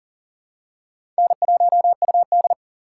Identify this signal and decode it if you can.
Morse code, heard as a keyed tone, about 39 wpm, 690 Hz. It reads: N1UD
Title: USA Call Signs - 4 Characters - 2x Character Spacing - 40wpm